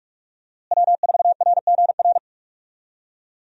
Morse code, heard as a keyed tone, about 46 wpm, 700 Hz. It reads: W4RGR
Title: USA Call Signs - 5 Characters - Repeated 3x - 45wpm